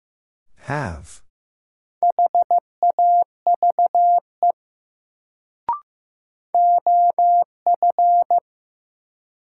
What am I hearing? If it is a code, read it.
Morse code, heard as a keyed tone, about 15 wpm, 700 Hz. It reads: HAVE OF
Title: Sets of 2 Words - New Word Drill - 15wpm